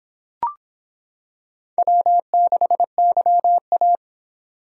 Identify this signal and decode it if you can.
Morse code, heard as a keyed tone, about 26 wpm, 700 Hz. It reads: W6YA